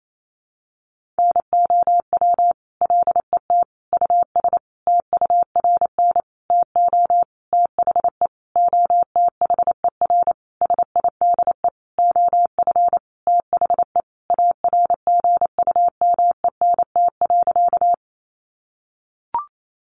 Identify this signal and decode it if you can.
Morse code, heard as a keyed tone, about 28 wpm, 700 Hz. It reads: NOW LET US TURN TO THE OTHER SIDE OF THE ARGUMENT.
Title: Sentences from Top 1400 Words 28wpm